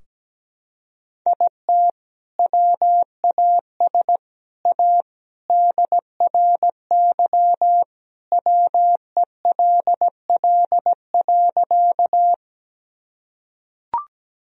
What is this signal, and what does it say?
Morse code, heard as a keyed tone, about 17 wpm, 700 Hz. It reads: IT WAS A DRY WELL.
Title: Sentences from Top 1000 Words - Review Part 3 - 17wpm